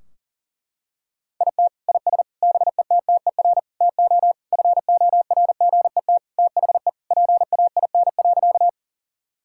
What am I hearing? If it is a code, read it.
Morse code, heard as a keyed tone, about 40 wpm, 700 Hz. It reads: IT IS BETTER TO FORGET THE PAIN.
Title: Sentences from Top 1600 Words 40wpm